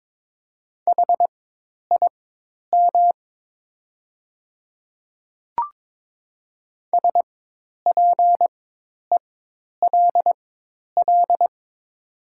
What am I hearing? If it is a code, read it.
Morse code, heard as a keyed tone, about 22 wpm, spaced as for 10 wpm, 690 Hz. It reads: HIM SPELL
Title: 4x Character Spacing - Top 200 Words - 22wpm